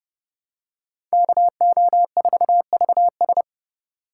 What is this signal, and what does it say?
Morse code, heard as a keyed tone, about 29 wpm, 700 Hz. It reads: KO4VS